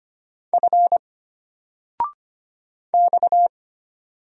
Morse code, heard as a keyed tone, about 25 wpm, 700 Hz. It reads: F X
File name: Single-Letter-Number-25wpm